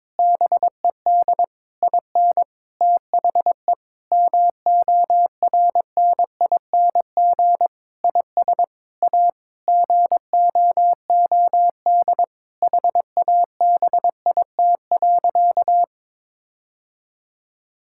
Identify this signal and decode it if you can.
Morse code, heard as a keyed tone, about 22 wpm, 700 Hz. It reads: BED IN THE MORNING IS A GOOD HABIT.
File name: Difficult Sentences with Hints - 22wpm